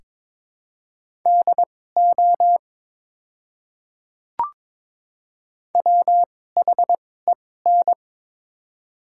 Morse code, heard as a keyed tone, about 21 wpm, spaced as for 16 wpm, 700 Hz. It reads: DO WHEN
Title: 2x Character Spacing - Top 300 Words - 22wpm